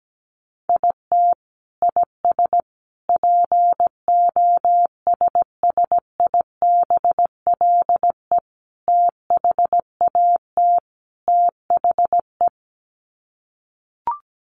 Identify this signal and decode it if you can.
Morse code, heard as a keyed tone, about 17 wpm, 700 Hz. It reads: IT IS POSSIBLE THAT THE
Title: Sets of 5 Words Encore2 17wpm